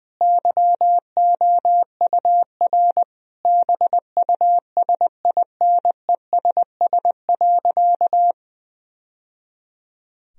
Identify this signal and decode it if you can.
Morse code, heard as a keyed tone, about 21 wpm, 700 Hz. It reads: YOUR BUSINESS.
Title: Sentences from Top 1000 Words - Review Part 4 - 20wpm